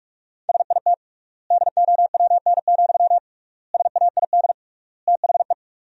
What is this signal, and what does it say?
Morse code, heard as a keyed tone, about 46 wpm, 700 Hz. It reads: SIT DOWN, SAID THE
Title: 3x Word-Spacing - Sentences from Top 500 Words - 45wpm